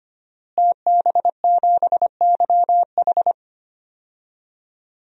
Morse code, heard as a keyed tone, about 25 wpm, 700 Hz. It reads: TB7YH
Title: DXCC-100 Call Signs - 5 Characters - Repeated 3x - 25wpm